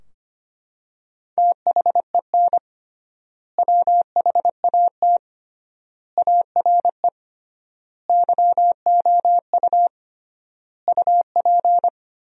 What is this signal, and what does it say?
Morse code, heard as a keyed tone, about 25 wpm, 700 Hz. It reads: THEN WHAT ARE YOU UP
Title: 3x Word-Spacing - Sentences from Top 100 Words - 25wpm